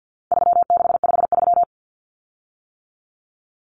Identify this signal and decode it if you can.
Morse code, heard as a keyed tone, about 50 wpm, 700 Hz. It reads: 3653
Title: Common Cut Numbers - 4-Digit 50wpm